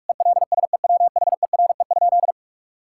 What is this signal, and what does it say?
Morse code, heard as a keyed tone, about 46 wpm, 700 Hz. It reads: ELSEWHERE?
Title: Sentences from Top 1600 Words 45wpm